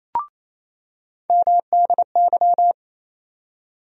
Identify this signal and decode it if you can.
Morse code, heard as a keyed tone, about 28 wpm, 700 Hz. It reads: MDY
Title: USA Call Sign Postfixes - 3 Letters 28wpm